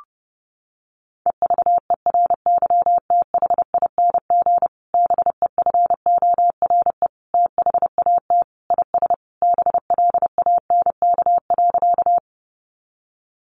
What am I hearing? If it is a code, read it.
Morse code, heard as a keyed tone, about 28 wpm, 700 Hz. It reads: EVERYTHING BEFORE THAT IS BLANK.
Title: Sentences from Top 1500 Words 30wpm